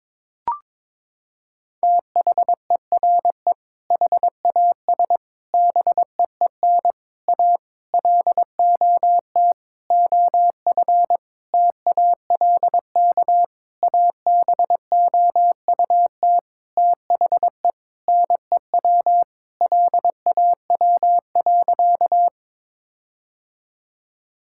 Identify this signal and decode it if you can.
Morse code, heard as a keyed tone, about 22 wpm, 700 Hz. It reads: THERE HAS BEEN A LOT OF TALK ABOUT THE NEW LAW.
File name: practice / 22wpm / Sentences with 8-Word Phrases - 22wpm